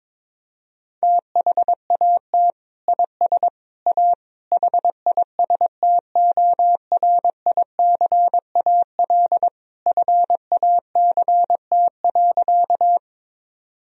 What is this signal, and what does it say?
Morse code, heard as a keyed tone, about 22 wpm, 700 Hz. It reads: THAT IS A HISTORICAL FACT.
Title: Sentences from Top 1000 Words - Review Part 4 - 22wpm